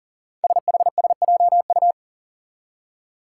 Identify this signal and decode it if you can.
Morse code, heard as a keyed tone, about 42 wpm, 700 Hz. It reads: SHSJU
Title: Five Challenging Letter-Number Combinations 40wpm